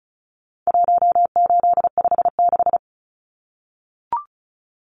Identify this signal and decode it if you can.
Morse code, heard as a keyed tone, about 34 wpm, 700 Hz. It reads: 1856
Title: Numbers - 4-Digit 35wpm